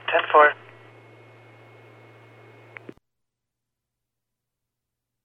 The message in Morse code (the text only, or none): none